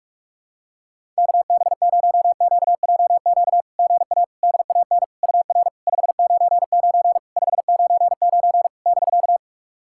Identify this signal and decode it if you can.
Morse code, heard as a keyed tone, about 46 wpm, 690 Hz. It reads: KB0QJQ GA DAN UR 599 599 <BK>
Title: SOTA QSO Practice 45wpm